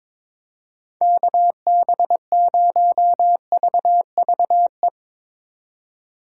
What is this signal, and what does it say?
Morse code, heard as a keyed tone, about 22 wpm, 700 Hz. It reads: KB0VVE
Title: USA Call Signs - 6 Characters - 4x Character Spacing - 22wpm